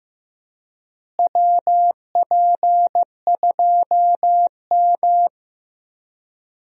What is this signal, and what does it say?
Morse code, heard as a keyed tone, about 16 wpm, 700 Hz. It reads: WP2M
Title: USA Call Signs - 4 Characters - Repeated 3x - 15wpm